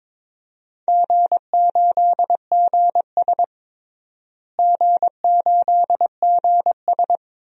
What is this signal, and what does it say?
Morse code, heard as a keyed tone, about 22 wpm, 700 Hz. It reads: G8GS G8GS
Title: DXCC-100 Call Signs - 4 Characters - Repeated 2x - 22wpm